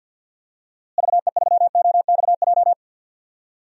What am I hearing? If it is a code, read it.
Morse code, heard as a keyed tone, about 48 wpm, 700 Hz. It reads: VE3OXJ
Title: DXCC-100 Call Signs - 6 Characters - Repeated 3x - 50wpm